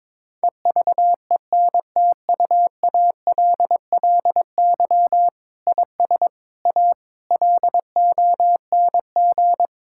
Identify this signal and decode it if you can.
Morse code, heard as a keyed tone, about 22 wpm, 700 Hz. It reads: EVENTUALLY IS A LONG
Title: Sentences from Top 1000 Words - Review Part 6 - 22wpm